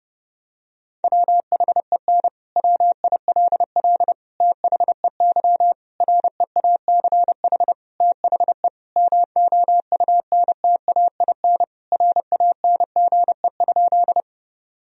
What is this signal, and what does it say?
Morse code, heard as a keyed tone, about 32 wpm, 700 Hz. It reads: WHEN WILL THEY REACH THE MOUNTAIN RANGE?
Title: Sentences from Top 1000 Words - Review Part 4 - 30wpm